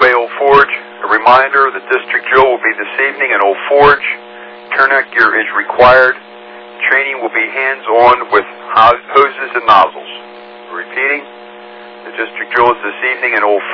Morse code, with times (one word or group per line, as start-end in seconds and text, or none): none